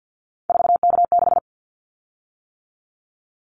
Morse code, heard as a keyed tone, about 50 wpm, 690 Hz. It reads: 4K6
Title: Callsign Prefixes for Top 100 DXCC - 50wpm